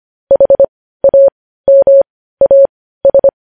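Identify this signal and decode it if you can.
Morse code, heard as a keyed tone, about 25 wpm, spaced as for 15 wpm, 550 Hz. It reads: HAMAS
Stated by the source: Morse code news headlines at 15 wpm.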